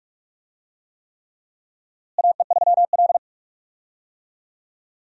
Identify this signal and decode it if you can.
Morse code, heard as a keyed tone, about 47 wpm, 700 Hz. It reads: AE3L
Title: USA Call Signs - 4 Characters - Repeated 3x - 45wpm